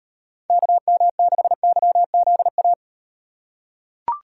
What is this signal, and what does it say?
Morse code, heard as a keyed tone, about 35 wpm, 700 Hz. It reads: KM6YZA